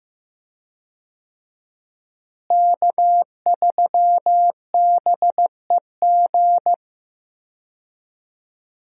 Morse code, heard as a keyed tone, about 15 wpm, 700 Hz. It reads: K3BEG
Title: USA Call Signs - 5 Characters - Repeated 2x - 15wpm